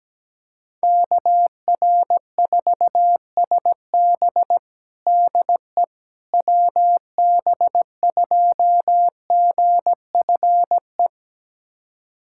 Morse code, heard as a keyed tone, about 17 wpm, 700 Hz. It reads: KR4SB DE WB2GFE
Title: USA Callsign DE Callsign 17wpm